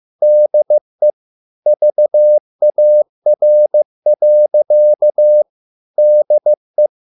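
Morse code, heard as a keyed tone, about 15 wpm, 600 Hz. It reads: DE VAR. DE
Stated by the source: Morse code at a slow 15 wpm from CW med Gnister.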